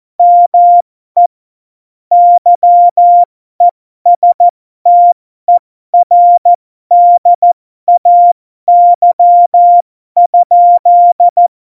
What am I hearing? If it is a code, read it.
Morse code, heard as a keyed tone, about 14 wpm, spaced as for 12 wpm, 700 Hz. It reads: ME YESTERDAY?
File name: Morsekurs / Witze